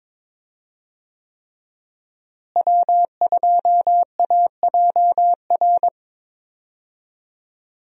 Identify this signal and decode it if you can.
Morse code, heard as a keyed tone, about 22 wpm, 700 Hz. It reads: W2AJR
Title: USA Call Signs - 5 Characters - Repeated 3x - 22wpm